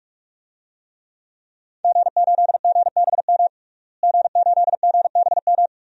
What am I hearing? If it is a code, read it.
Morse code, heard as a keyed tone, about 42 wpm, 700 Hz. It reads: G8GBM G8GBM
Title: DXCC-100 Call Signs - 5 Characters - Repeated 2x - 45wpm